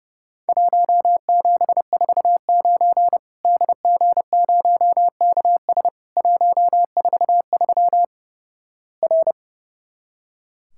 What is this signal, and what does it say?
Morse code, heard as a keyed tone, about 30 wpm, 700 Hz. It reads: 1749 DG0KS 143 R
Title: Worked All Europe DX Contest - QTC Exchange - 30wpm